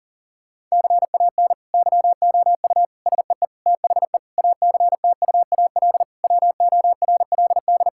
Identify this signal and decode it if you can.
Morse code, heard as a keyed tone, about 40 wpm, 700 Hz. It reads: CAN YOU SEE THE ACTUAL WORLD
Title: Sentences from Top 1000 Words - Review Part 5 - 40wpm